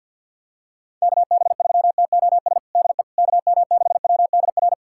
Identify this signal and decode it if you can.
Morse code, heard as a keyed tone, about 51 wpm, 700 Hz. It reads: KB3TOS DE KM6WDR
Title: USA Callsign DE Callsign - 6 Characters - 50wpm